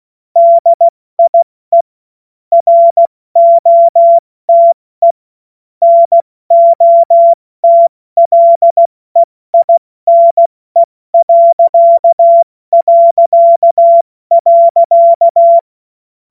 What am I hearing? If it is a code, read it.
Morse code, heard as a keyed tone, about 16 wpm, 680 Hz. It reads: DIE ROTE NOTLEINE...
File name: Morsekurs / Witze / Witz 31.Oktober 2023 70 F80